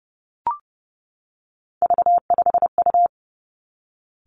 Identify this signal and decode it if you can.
Morse code, heard as a keyed tone, about 30 wpm, 700 Hz. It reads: V5U
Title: Three Challenging Letter-Number Combinations 30wpm